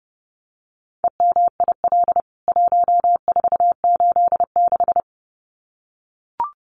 Morse code, heard as a keyed tone, about 30 wpm, 700 Hz. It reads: EMIL 1486
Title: CWT Exchange 30wpm